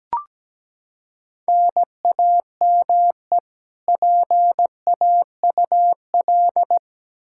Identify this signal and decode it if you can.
Morse code, heard as a keyed tone, about 17 wpm, 700 Hz. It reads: NAME PAUL
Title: Practice QSO - Exchange 1 - Beginning - 17wpm